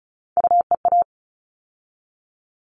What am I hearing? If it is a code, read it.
Morse code, heard as a keyed tone, about 34 wpm, 700 Hz. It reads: UEA
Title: All Cut Numbers - 3-Digit 35wpm